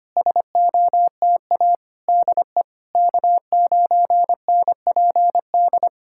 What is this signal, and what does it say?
Morse code, heard as a keyed tone, about 25 wpm, 700 Hz. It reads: SOTA DE K9NPD
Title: SOTA QSO Practice 25wpm